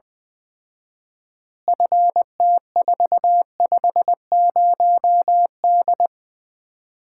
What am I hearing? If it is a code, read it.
Morse code, heard as a keyed tone, about 20 wpm, 700 Hz. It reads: FT450D
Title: Common QSO Phrases - 4x Word-Spacing - 20wpm